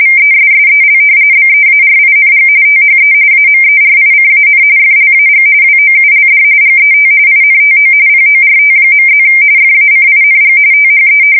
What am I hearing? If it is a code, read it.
RTTY, about 45.45 baud. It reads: VESSEL BEARING THE AMERICAN FLAG WOULD GET INTO PORT.+ FACING THE
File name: fsk170 / battleships